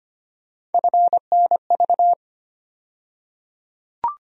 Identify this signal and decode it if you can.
Morse code, heard as a keyed tone, about 25 wpm, 700 Hz. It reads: FNV
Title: USA Call Sign Postfixes - 3 Letters 25wpm